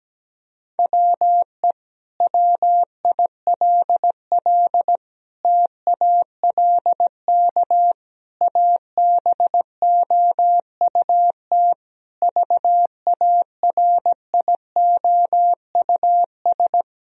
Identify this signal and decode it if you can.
Morse code, heard as a keyed tone, about 17 wpm, 700 Hz. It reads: WE WILL TALK ABOUT VARIOUS
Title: Sentences from Top 1000 Words - Review Part 4 - 17wpm